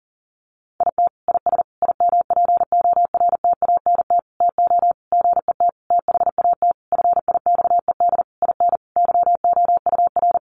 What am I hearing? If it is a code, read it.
Morse code, heard as a keyed tone, about 40 wpm, 700 Hz. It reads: IT IS IMPORTANT TO GET THAT FIXED IN YOUR
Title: Sentences from Top 1100 Words 40wpm